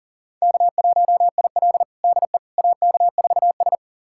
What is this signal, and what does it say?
Morse code, heard as a keyed tone, about 37 wpm, 700 Hz. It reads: K1IL DE AK4S
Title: USA Callsign DE Callsign - 4 Characters - 40wpm